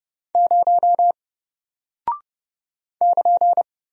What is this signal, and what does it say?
Morse code, heard as a keyed tone, about 30 wpm, 700 Hz. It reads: 0 <KN>
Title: Single Character (Letter:Number:Punctuation:Prosign) Alt 30wpm